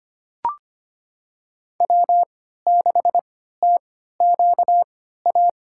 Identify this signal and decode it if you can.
Morse code, heard as a keyed tone, about 25 wpm, spaced as for 14 wpm, 700 Hz. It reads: W6TQA
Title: USA Call Signs - 5 Characters - 3x Character Spacing - 25wpm